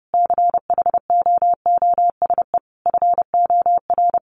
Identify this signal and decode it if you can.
Morse code, heard as a keyed tone, about 30 wpm, 700 Hz. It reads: CHOOSE FOR
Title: Sentences from Top 1500 Words 30wpm